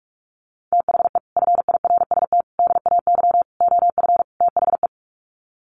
Morse code, heard as a keyed tone, about 44 wpm, 700 Hz. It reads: THE FIRST DAY OF THE
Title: Sets of 5 Words Encore2 45wpm